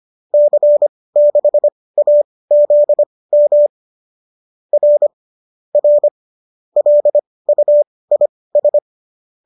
Morse code, heard as a keyed tone, about 25 wpm, spaced as for 18 wpm, 570 Hz. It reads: C6AZM R R LUIS
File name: qso204 18